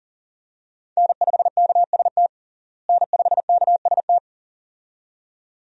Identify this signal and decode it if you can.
Morse code, heard as a keyed tone, about 40 wpm, 700 Hz. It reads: N5KST N5KST
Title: USA Call Signs - 5 Characters - Repeated 2x - 40wpm